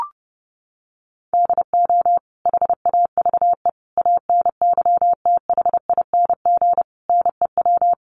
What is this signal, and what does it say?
Morse code, heard as a keyed tone, about 30 wpm, 700 Hz. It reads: DO HAVE ANYTHING NEW
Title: Sentences from Top 1300 Words 30wpm